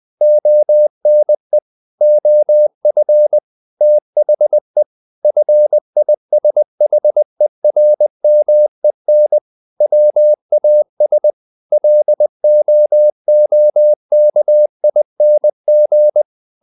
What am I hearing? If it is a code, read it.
Morse code, heard as a keyed tone, about 20 wpm, 600 Hz. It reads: ONE OF THE FISHERMEN WAS LOOKING
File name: the old man and the sea part 2 20wpm